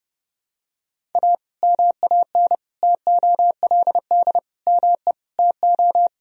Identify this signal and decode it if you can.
Morse code, heard as a keyed tone, about 31 wpm, 700 Hz. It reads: A MAN TOLD ME TO
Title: Sentences from Top 1000 Words - Review Part 3 - 30wpm